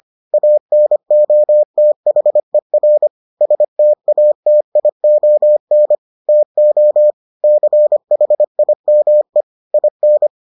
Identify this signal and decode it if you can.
Morse code, heard as a keyed tone, about 25 wpm, 600 Hz. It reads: ANOTHER STATION TO CHIME IN